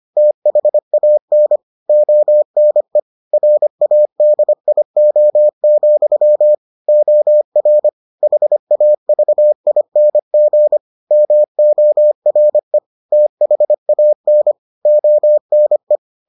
Morse code, heard as a keyed tone, about 25 wpm, 600 Hz. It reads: THAN ONE RADIO, OR HAVING MORE THAN ONE